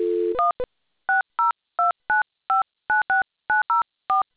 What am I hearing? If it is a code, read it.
DTMF, heard as telephone touch tones, tones of about 150 ms. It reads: 1 6029596904